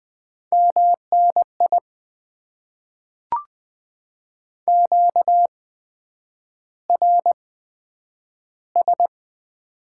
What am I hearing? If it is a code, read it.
Morse code, heard as a keyed tone, about 20 wpm, spaced as for 9 wpm, 700 Hz. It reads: MNI QRS
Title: Top 90 QSO Elements - 8x Character Spacing - 20wpm